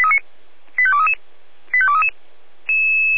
Five-tone selective call (ZVEI), heard as quick runs of five tones, tones of about 70 ms. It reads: 85233 85233